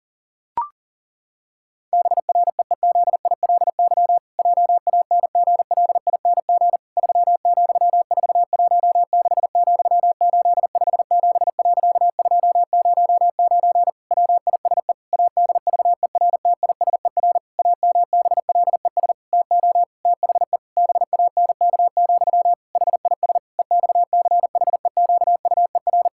Morse code, heard as a keyed tone, about 40 wpm, 700 Hz. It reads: BREEZILY JANGLING 3,416,857.209 WISE ADVERTISER AMBLES TO THE BANK, HIS EXCHEQUER
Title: Pangram Challenge v2 - 40wpm